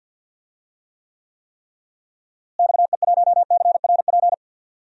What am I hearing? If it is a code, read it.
Morse code, heard as a keyed tone, about 52 wpm, 700 Hz. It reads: XE1CRP
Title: DXCC-100 Call Signs - 6 Characters - Repeated 3x - 50wpm